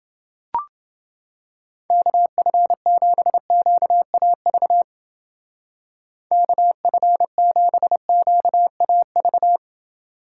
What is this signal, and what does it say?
Morse code, heard as a keyed tone, about 28 wpm, 700 Hz. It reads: KF7QAV KF7QAV